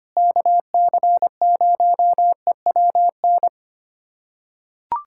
Morse code, heard as a keyed tone, about 25 wpm, 700 Hz. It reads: KC0EWN
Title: USA Callsign DE Callsign 25wpm